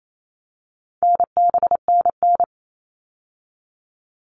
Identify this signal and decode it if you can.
Morse code, heard as a keyed tone, about 28 wpm, 700 Hz. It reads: NBNN
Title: All Cut Numbers - 4-Digit 28wpm